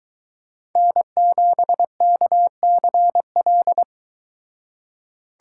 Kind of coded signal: Morse code

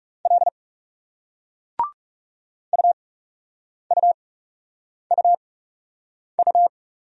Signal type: Morse code

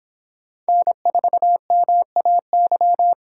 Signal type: Morse code

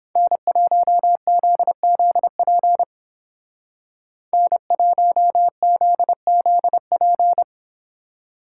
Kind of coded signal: Morse code

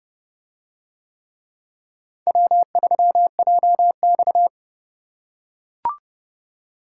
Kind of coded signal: Morse code

